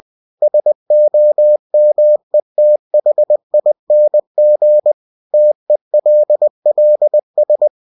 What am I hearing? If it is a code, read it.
Morse code, heard as a keyed tone, about 20 wpm, 600 Hz. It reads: SOMETHING TELLS
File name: cw-books / winnie-the-pooh / winnie-20wpm-page-0007